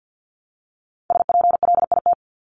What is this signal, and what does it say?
Morse code, heard as a keyed tone, about 48 wpm, 700 Hz. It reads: SPLIT
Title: Top 1751-2000 Words 50wpm